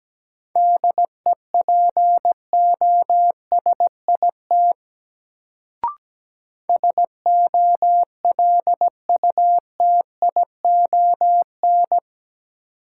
Morse code, heard as a keyed tone, about 17 wpm, 700 Hz. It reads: DEPOSIT SOLUTION
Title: Sentences from Top 1300 Words - New Word Drill - 17wpm